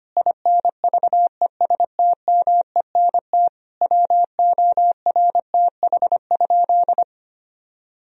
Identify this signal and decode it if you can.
Morse code, heard as a keyed tone, about 25 wpm, 700 Hz. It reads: INVESTMENT WORTH?
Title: Sentences from Top 1000 Words - Review Part 4 - 25wpm